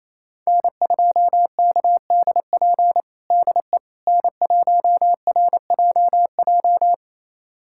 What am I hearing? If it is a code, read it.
Morse code, heard as a keyed tone, about 28 wpm, 700 Hz. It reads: N2KDP DE N1RJJ
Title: USA Callsign DE Callsign - 5 Characters - 28wpm